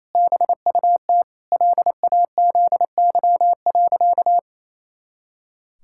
Morse code, heard as a keyed tone, about 28 wpm, 700 Hz. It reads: BUT LAZY.
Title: Pangram Challenge v2 - 28wpm